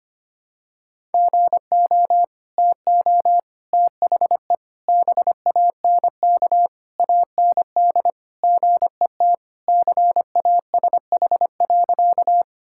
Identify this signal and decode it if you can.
Morse code, heard as a keyed tone, about 25 wpm, 700 Hz. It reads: GO TO THE BANK AND GET CASH.
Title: Sentences from Top 1000 Words - Review Part 5 - 25wpm